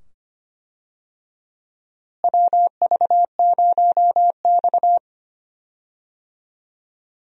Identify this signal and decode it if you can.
Morse code, heard as a keyed tone, about 25 wpm, 700 Hz. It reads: WV0X
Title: USA Call Signs - 4 Characters - Repeated 2x - 25wpm